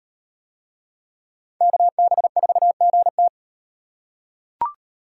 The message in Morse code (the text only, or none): KB4GT